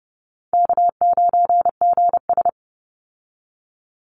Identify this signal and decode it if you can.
Morse code, heard as a keyed tone, about 30 wpm, 700 Hz. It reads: K9GS
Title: Call Signs Repeated 2x - 30wpm